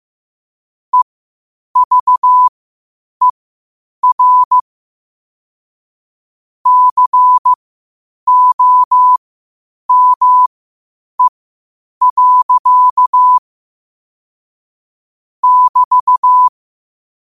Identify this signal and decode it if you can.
Morse code, heard as a keyed tone, about 15 wpm, spaced as for 8 wpm, 1000 Hz. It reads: EVER COME. <BT>
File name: qotdfiles / 8wpm